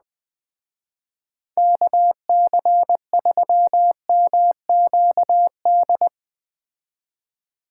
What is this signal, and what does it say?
Morse code, heard as a keyed tone, about 20 wpm, 700 Hz. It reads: KC3MQD